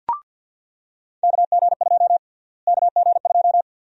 Morse code, heard as a keyed tone, about 45 wpm, 700 Hz. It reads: KG2 KG2
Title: USA Call Sign Prefixes - Repeated 2x - 50wpm